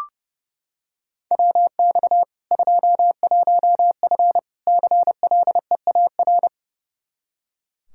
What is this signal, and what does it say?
Morse code, heard as a keyed tone, about 30 wpm, 700 Hz. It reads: WX 21F CLEAR
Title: K1USN SST QSO Practice 30wpm